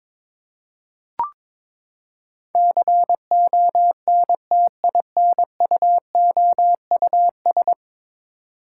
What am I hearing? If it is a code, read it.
Morse code, heard as a keyed tone, about 22 wpm, 700 Hz. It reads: CONTINUOUS
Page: morse-code-ninja-practice-22wpm